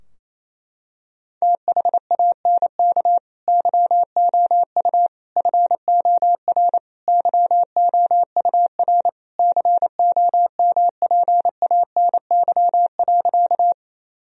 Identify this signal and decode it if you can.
Morse code, heard as a keyed tone, about 28 wpm, 700 Hz. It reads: THANK YOU FOR YOUR COMPANY.
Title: Sentences from Top 1000 Words - Review Part 3 - 28wpm